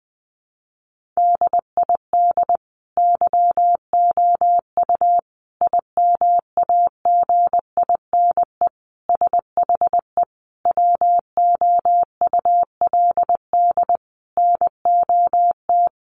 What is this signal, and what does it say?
Morse code, heard as a keyed tone, about 20 wpm, 700 Hz. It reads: DID YOU IMAGINE SHE WOULD NOT
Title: Sentences from Top 1400 Words 20wpm